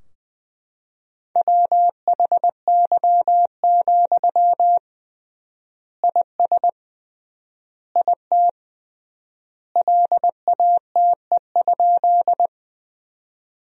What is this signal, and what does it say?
Morse code, heard as a keyed tone, about 20 wpm, 700 Hz. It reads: WHY, IS IT LATE?
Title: 3x Word-Spacing - Sentences from Top 300 Words - 20wpm